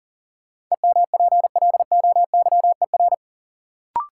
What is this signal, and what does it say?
Morse code, heard as a keyed tone, about 40 wpm, 700 Hz. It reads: EMPLOYER
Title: Sentences from Top 1600 Words - New Word Drill - 40wpm